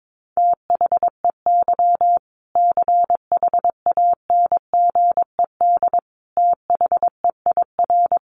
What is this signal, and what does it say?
Morse code, heard as a keyed tone, about 22 wpm, 700 Hz. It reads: THEY CHANGED THEIR
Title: A Sound Of Thunder 22wpm